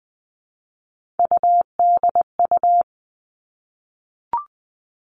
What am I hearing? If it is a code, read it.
Morse code, heard as a keyed tone, about 20 wpm, 700 Hz. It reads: UDU
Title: Three Challenging Letter-Number Combinations 20wpm